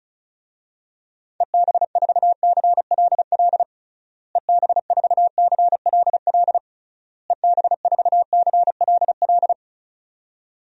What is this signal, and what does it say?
Morse code, heard as a keyed tone, about 35 wpm, 700 Hz. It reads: EB4CLL EB4CLL EB4CLL
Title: DXCC-100 Call Signs - 6 Characters - Repeated 3x - 35wpm